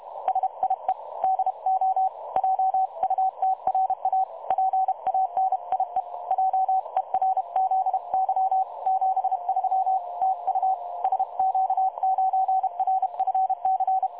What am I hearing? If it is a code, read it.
Morse code, heard as a keyed tone, about 33 wpm, 760 Hz. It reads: SIE DO JUTRA PANIE JERZY 73 TU SQ9RFC
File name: dlazaawansowanych / nagraniazpasma / websdr recording 3565.0kHz